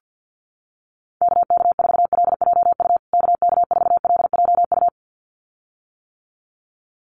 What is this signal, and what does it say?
Morse code, heard as a keyed tone, about 50 wpm, 700 Hz. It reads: KK4LJU KK4LJU
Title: USA Call Signs Repeated 2x 50wpm